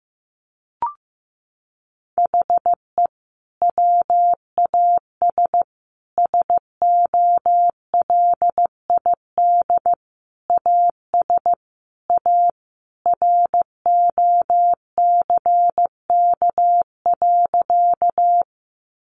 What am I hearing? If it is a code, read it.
Morse code, heard as a keyed tone, about 15 wpm, 700 Hz. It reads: HE WAS SOLID AS A ROCK.
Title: Sentences from Top 1500 Words 15wpm